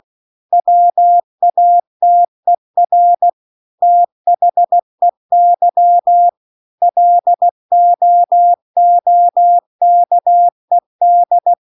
Morse code, heard as a keyed tone, about 16 wpm, 700 Hz. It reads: WATER THEY LOOKED
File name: CW-The-Big-Twoheaded-River-2-16WPM